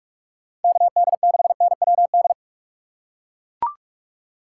Morse code, heard as a keyed tone, about 44 wpm, 700 Hz. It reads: KD6NWD